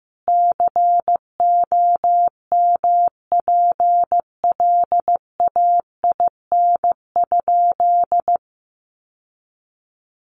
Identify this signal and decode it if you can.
Morse code, heard as a keyed tone, about 15 wpm, 700 Hz. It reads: COMPLAIN?
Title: Sentences from Top 1300 Words 15wpm